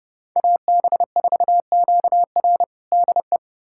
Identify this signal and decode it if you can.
Morse code, heard as a keyed tone, about 30 wpm, 700 Hz. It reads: AB4QR DE
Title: USA Callsign DE Callsign - 5 Characters - 30wpm